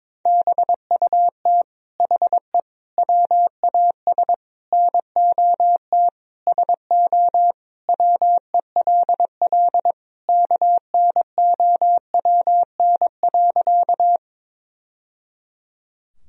Morse code, heard as a keyed tone, about 22 wpm, 700 Hz. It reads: BUT HE WAS NOT SO WELL KNOWN.
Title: Sentences from Top 1000 Words - Review Part 3 - 22wpm